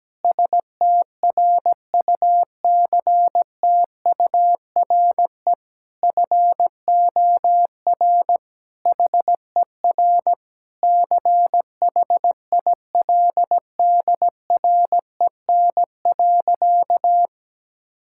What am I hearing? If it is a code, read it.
Morse code, heard as a keyed tone, about 17 wpm, 700 Hz. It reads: STRUCTURE FOR HER CHILDREN.
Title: Sentences from Top 1000 Words - Review Part 6 - 17wpm